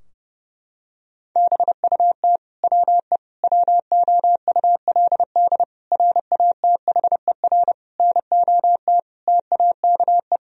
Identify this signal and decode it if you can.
Morse code, heard as a keyed tone, about 30 wpm, 700 Hz. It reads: BUT WE WOULD RATHER NOT TAKE
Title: Sentences from Top 1000 Words - Review Part 4 - 30wpm